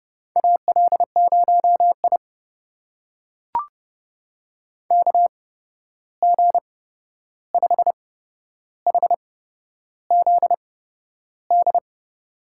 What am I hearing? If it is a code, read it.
Morse code, heard as a keyed tone, about 30 wpm, spaced as for 11 wpm, 700 Hz. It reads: AL0I KG5HZD
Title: USA Call Signs - 8x Character Spacing - 30wpm